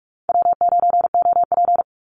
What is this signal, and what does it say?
Morse code, heard as a keyed tone, about 46 wpm, 700 Hz. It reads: W9OP
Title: Call Signs - 45wpm